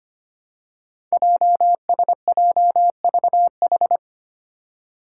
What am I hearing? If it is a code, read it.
Morse code, heard as a keyed tone, about 25 wpm, 700 Hz. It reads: JSJVH
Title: Five Challenging Letter-Number Combinations 25wpm